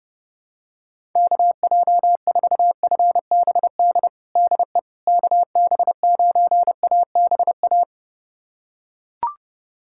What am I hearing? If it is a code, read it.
Morse code, heard as a keyed tone, about 30 wpm, 700 Hz. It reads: KJ4FBD DE KB9ABA
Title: USA Callsign DE Callsign - 6 Characters - 30wpm